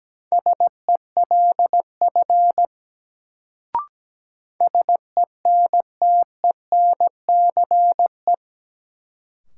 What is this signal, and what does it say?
Morse code, heard as a keyed tone, about 17 wpm, 700 Hz. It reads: SELF SENTENCE
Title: Sentences from Top 1000 Words - Review Part 1 - New Word Drill - 17wpm